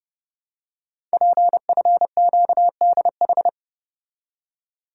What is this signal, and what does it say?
Morse code, heard as a keyed tone, about 30 wpm, 700 Hz. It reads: PFQDH
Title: Code Groups - 5 Letters - 30wpm